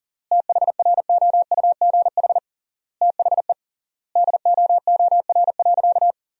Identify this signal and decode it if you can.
Morse code, heard as a keyed tone, about 40 wpm, 700 Hz. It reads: THROUGH THE DOOR.
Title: 3x Word-Spacing - Sentences from Top 400 Words - 40wpm